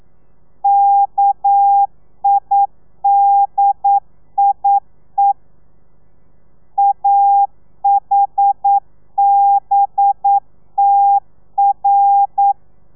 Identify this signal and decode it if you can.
Morse code, heard as a keyed tone, about 9 wpm, 800 Hz. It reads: KIDIE AHBTR
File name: telegrafia / i2tfj / Lez11 40